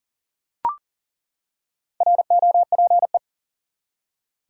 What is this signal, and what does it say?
Morse code, heard as a keyed tone, about 40 wpm, 700 Hz. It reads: ROPE
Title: Four Letter Words - 40wpm